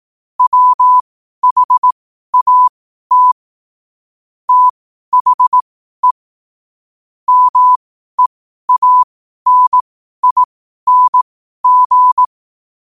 Morse code, heard as a keyed tone, about 18 wpm, spaced as for 12 wpm, 1000 Hz. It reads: WHAT THE MEANING
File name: qotdfiles / 12wpm